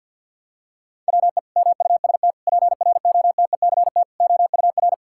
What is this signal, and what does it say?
Morse code, heard as a keyed tone, about 52 wpm, 700 Hz. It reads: WE MUST PROTECT OUR
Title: Sentences from Top 1000 Words - Review Part 5 - 50wpm